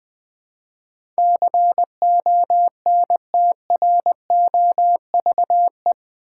Morse code, heard as a keyed tone, about 20 wpm, 700 Hz. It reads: CONTROVE
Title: Build A Word - 4 Letters At A Time - 11-Letter Words - 20wpm